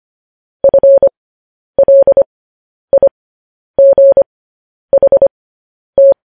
Morse code, heard as a keyed tone, about 25 wpm, spaced as for 10 wpm, 550 Hz. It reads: FLIGHT
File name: podcast / News headlines-1x-25x10